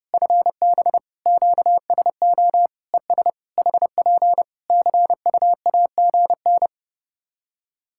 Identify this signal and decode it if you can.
Morse code, heard as a keyed tone, about 29 wpm, 700 Hz. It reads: FB QSO ES HP CUAGN
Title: Practice QSO - Exchange 4 - Ending - 30wpm